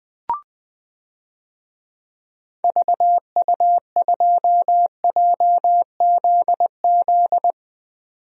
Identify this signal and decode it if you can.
Morse code, heard as a keyed tone, about 20 wpm, 700 Hz. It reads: VU2JZZ
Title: DXCC-100 Call Signs - 6 Characters - 20wpm